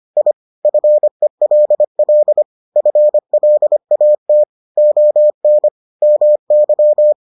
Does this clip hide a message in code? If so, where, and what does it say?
Yes: Morse code from 0.2 to 7.1 s, heard as a keyed tone, about 25 wpm, 600 Hz. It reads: I FELL FLAT ON MY